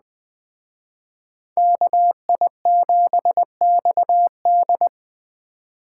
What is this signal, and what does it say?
Morse code, heard as a keyed tone, about 21 wpm, 700 Hz. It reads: KI7XD
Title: Speed Racing - USA Call Signs 17wpm